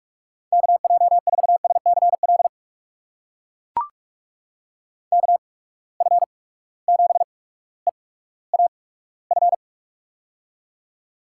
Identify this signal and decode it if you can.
Morse code, heard as a keyed tone, about 43 wpm, spaced as for 19 wpm, 700 Hz. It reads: KJ4SCL KF7EAF
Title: USA Call Signs - 6 Characters - 8x Character Spacing - 45wpm